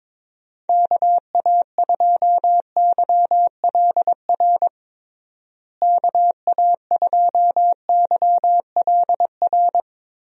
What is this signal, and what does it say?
Morse code, heard as a keyed tone, about 22 wpm, 700 Hz. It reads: KA2YLR KA2YLR
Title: USA Call Signs - 6 Characters - Repeated 3x - 22wpm